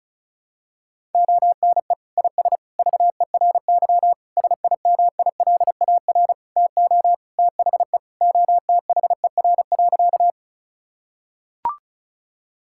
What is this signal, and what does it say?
Morse code, heard as a keyed tone, about 36 wpm, 700 Hz. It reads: ONE IS VERY SIMILAR TO THE OTHER.
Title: Sentences from Top 1000 Words - Review Part 5 - 35wpm